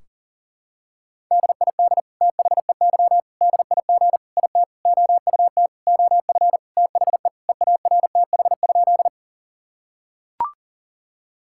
Morse code, heard as a keyed tone, about 40 wpm, 700 Hz. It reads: DID THEY DIG IT OUT OF THE EARTH?
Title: Sentences from Top 1600 Words 40wpm